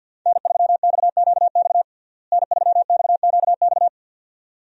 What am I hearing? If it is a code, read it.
Morse code, heard as a keyed tone, about 46 wpm, 700 Hz. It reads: N3XQX N3XQX
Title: USA Call Signs - 5 Characters - Repeated 2x - 50wpm